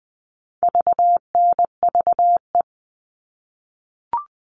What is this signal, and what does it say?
Morse code, heard as a keyed tone, about 20 wpm, 700 Hz. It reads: VNVE
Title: All Cut Numbers - 4-Digit 20wpm